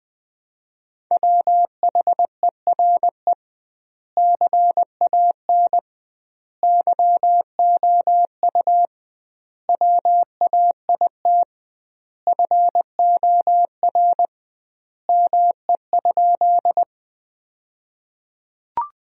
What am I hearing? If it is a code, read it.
Morse code, heard as a keyed tone, about 21 wpm, 700 Hz. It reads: WHERE CAN YOU WAIT FOR ME?
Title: 2x Word-Spacing - Sentences from Top 500 Words - 20wpm